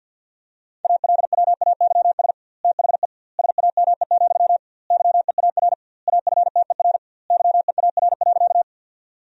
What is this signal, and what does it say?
Morse code, heard as a keyed tone, about 50 wpm, 700 Hz. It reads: ALWAYS THE SAME, YEAR AFTER YEAR.
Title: 2x Word-Spacing - Sentences from Top 300 Words - 50wpm